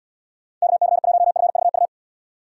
Morse code, heard as a keyed tone, about 78 wpm, 700 Hz. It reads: KC1FVU